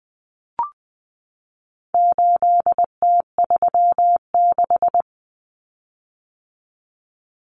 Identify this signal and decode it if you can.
Morse code, heard as a keyed tone, about 20 wpm, 700 Hz. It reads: 8T36
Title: Common Cut Numbers - 4-Digit 20wpm